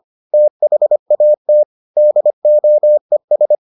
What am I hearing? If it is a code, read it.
Morse code, heard as a keyed tone, about 25 wpm, 600 Hz. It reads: THAT DOES